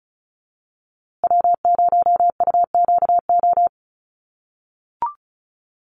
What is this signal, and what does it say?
Morse code, heard as a keyed tone, about 34 wpm, 700 Hz. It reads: W0UQO
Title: USA Call Signs Repeated 2x 35wpm